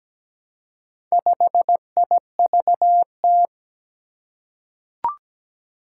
Morse code, heard as a keyed tone, about 17 wpm, 700 Hz. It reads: 5IVT
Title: Four Challenging Letter-Number Combinations 17wpm